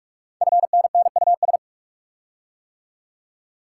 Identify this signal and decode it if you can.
Morse code, heard as a keyed tone, about 44 wpm, 700 Hz. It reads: FNNUS
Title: Code Groups - 5 Letters - 45wpm